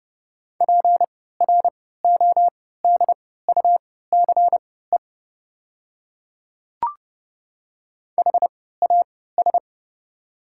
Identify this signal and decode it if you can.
Morse code, heard as a keyed tone, about 30 wpm, spaced as for 17 wpm, 690 Hz. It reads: PRODUCE HAS
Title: 3x Character Spacing - Top 500 Words - 30wpm